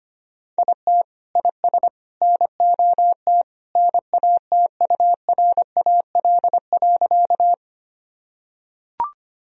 Morse code, heard as a keyed tone, about 25 wpm, 700 Hz. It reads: IT IS NOT NATURAL.
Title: Sentences from Top 1000 Words - Review Part 4 - 25wpm